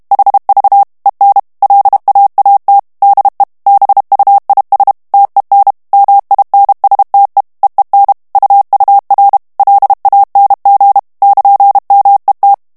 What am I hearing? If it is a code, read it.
Morse code, heard as a keyed tone, about 32 wpm, 800 Hz. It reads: HV EN LAAT DE BUIS TEN MINSTE EEN UUR LANG <KN>MET
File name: HOLLANDAIS texte clair 2 vitesse 30 mots minute